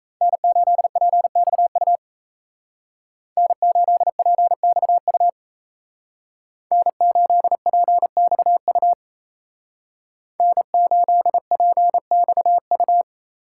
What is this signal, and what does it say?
Morse code, heard as a keyed tone, about 35 wpm, 700 Hz. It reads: N8PXU N8PXU N8PXU N8PXU